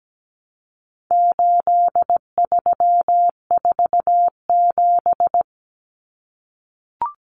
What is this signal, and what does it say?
Morse code, heard as a keyed tone, about 17 wpm, 700 Hz. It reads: 8347
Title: Numbers - 4-Digit 17wpm